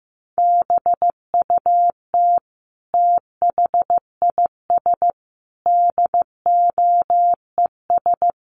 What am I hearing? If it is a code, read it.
Morse code, heard as a keyed tone, about 15 wpm, 700 Hz. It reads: BUT THIS DOES
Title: Sentences from Top 1400 Words 15wpm